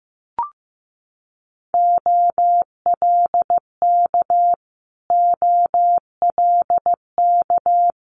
Morse code, heard as a keyed tone, about 15 wpm, 700 Hz. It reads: OLK OLK
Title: Multi-letter Phonemes Repeated 3x 15wpm